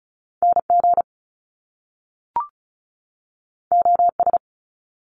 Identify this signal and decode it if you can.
Morse code, heard as a keyed tone, about 36 wpm, 700 Hz. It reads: NG OS
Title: Top Two Letter Combinations in Words 35wpm